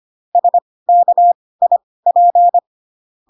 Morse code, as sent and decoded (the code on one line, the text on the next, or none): ... -.- .. .--.
SKIP